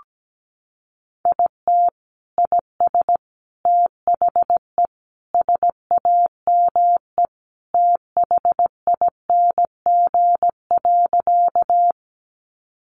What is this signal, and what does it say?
Morse code, heard as a keyed tone, about 17 wpm, 700 Hz. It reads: IT IS THE SAME THING.
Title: Sentences from Top 200 Words 17wpm